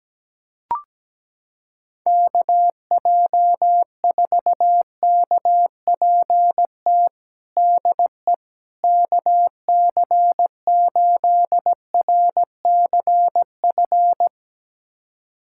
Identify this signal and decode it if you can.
Morse code, heard as a keyed tone, about 17 wpm, 700 Hz. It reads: KJ4KPT DE KC8RCF
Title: USA Callsign DE Callsign - 6 Characters - 17wpm